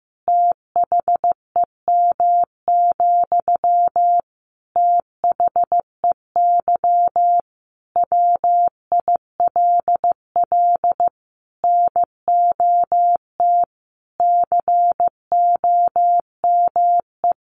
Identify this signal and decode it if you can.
Morse code, heard as a keyed tone, about 15 wpm, 700 Hz. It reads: THEM, THEY WILL NOT COME
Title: Sentences from Top 1300 Words 15wpm